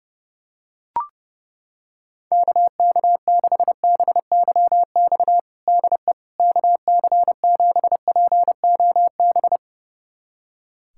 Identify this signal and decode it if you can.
Morse code, heard as a keyed tone, about 30 wpm, 700 Hz. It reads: KK6BYX DE KC7POB
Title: USA Callsign DE Callsign - 6 Characters - 30wpm